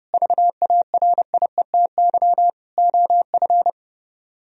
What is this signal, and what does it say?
Morse code, heard as a keyed tone, about 37 wpm, 700 Hz. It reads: VARIETY OF
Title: Speed-Racing - Sets of 4 Words 30wpm